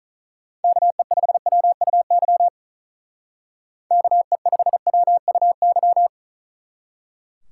Morse code, heard as a keyed tone, about 45 wpm, 700 Hz. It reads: KE5WUY KE5WUY